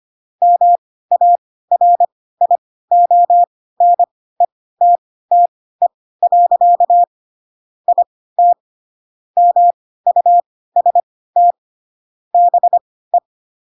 Morse code, heard as a keyed tone, about 25 wpm, spaced as for 16 wpm, 700 Hz. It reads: MARIONETTE. IT MUST BE
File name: Pinocchio / Pinocchio ch2 10 25 ik1hge